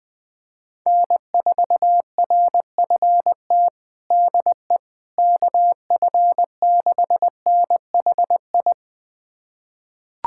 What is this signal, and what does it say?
Morse code, heard as a keyed tone, about 20 wpm, 700 Hz. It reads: N4RFT DE KF6NHI
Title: USA Callsign DE Callsign 20wpm